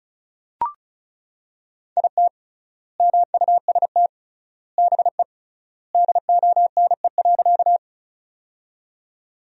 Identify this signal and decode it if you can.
Morse code, heard as a keyed tone, about 34 wpm, 700 Hz. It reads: IT MUST BE DONE.
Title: 3x Word-Spacing - Sentences from Top 500 Words - 35wpm